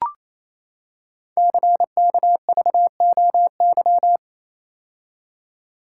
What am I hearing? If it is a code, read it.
Morse code, heard as a keyed tone, about 28 wpm, 700 Hz. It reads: CKVOY
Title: Code Groups - 5 Letters - 28wpm